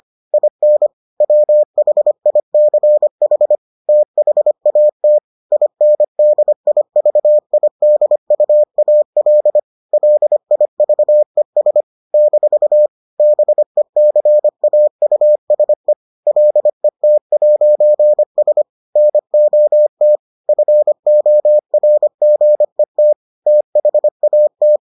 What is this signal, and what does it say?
Morse code, heard as a keyed tone, about 25 wpm, 600 Hz. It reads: IN WHICH THAT INDIVIDUAL LIVES - BECAUSE LET'S NOT FORGET THAT